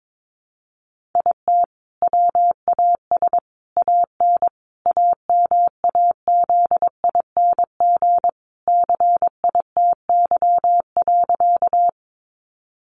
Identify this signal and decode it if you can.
Morse code, heard as a keyed tone, about 22 wpm, 700 Hz. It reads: IT WAS AN AMAZING CITY.
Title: Sentences from Top 1500 Words 22wpm